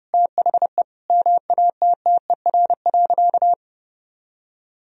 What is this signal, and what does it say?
Morse code, heard as a keyed tone, about 30 wpm, 700 Hz. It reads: THE MATTER.
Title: Sentences from Top 1000 Words - Review Part 3 - 30wpm